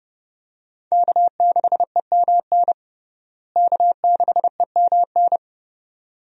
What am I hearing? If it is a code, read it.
Morse code, heard as a keyed tone, about 30 wpm, 700 Hz. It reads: K6EMN K6EMN
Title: USA Call Signs - 5 Characters - Repeated 2x - 30wpm